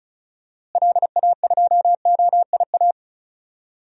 Morse code, heard as a keyed tone, about 36 wpm, 700 Hz. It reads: LA2OIA
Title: DXCC-100 Call Signs - 6 Characters - Repeated 2x - 35wpm